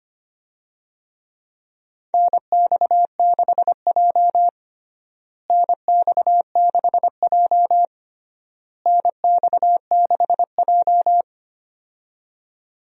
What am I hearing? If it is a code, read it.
Morse code, heard as a keyed tone, about 25 wpm, 700 Hz. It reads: NX6J NX6J NX6J
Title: USA Call Signs - 4 Characters - Repeated 3x - 25wpm